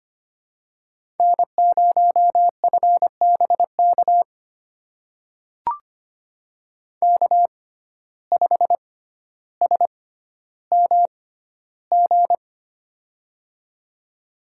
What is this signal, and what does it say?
Morse code, heard as a keyed tone, about 25 wpm, spaced as for 13 wpm, 700 Hz. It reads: N0FBK K5SMG
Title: USA Call Signs - 5 Characters - 6x Character Spacing - 25wpm